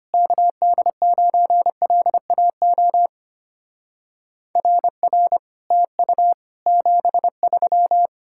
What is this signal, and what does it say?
Morse code, heard as a keyed tone, about 33 wpm, 700 Hz. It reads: KD9LAO RR TU 73